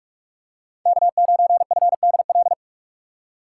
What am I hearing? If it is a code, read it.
Morse code, heard as a keyed tone, about 43 wpm, 700 Hz. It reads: K9FDL